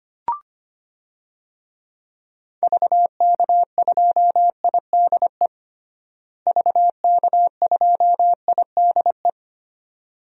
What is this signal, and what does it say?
Morse code, heard as a keyed tone, about 25 wpm, 700 Hz. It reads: VK2IDE VK2IDE
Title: DXCC-100 Call Signs - 6 Characters - Repeated 2x - 25wpm